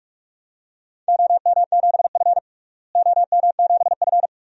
Morse code, heard as a keyed tone, about 46 wpm, 700 Hz. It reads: OM7F OM7F
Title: DXCC-100 Call Signs - 4 Characters - Repeated 2x - 45wpm